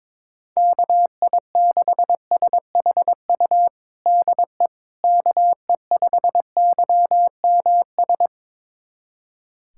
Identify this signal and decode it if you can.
Morse code, heard as a keyed tone, about 22 wpm, 700 Hz. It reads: KI6SHU DE KE5YMS
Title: USA Callsign DE Callsign - 6 Characters - 22wpm